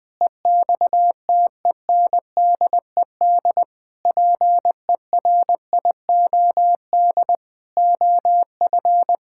Morse code, heard as a keyed tone, about 20 wpm, 700 Hz. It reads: EXTENDED PERIOD OF
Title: Sentences with 6-Word Phrases - Spoken Beginning - 20wpm